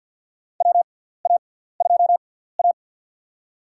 Morse code, heard as a keyed tone, about 52 wpm, spaced as for 17 wpm, 700 Hz. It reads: WA2A
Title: USA Call Signs - 4 Characters - 6x Character Spacing - 50wpm